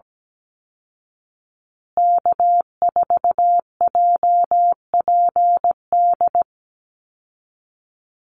Morse code, heard as a keyed tone, about 17 wpm, 700 Hz. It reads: K4JPD
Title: Call Signs Repeated 2x - 17wpm